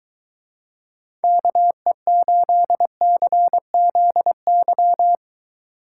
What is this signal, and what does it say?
Morse code, heard as a keyed tone, about 23 wpm, 700 Hz. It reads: KE8CZY